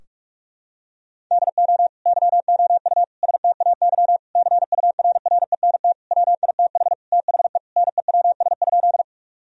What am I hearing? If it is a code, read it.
Morse code, heard as a keyed tone, about 41 wpm, 700 Hz. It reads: DO YOU STAY CURRENT WITH THE NEWS?
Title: Sentences from Top 1000 Words - Review Part 4 - 45wpm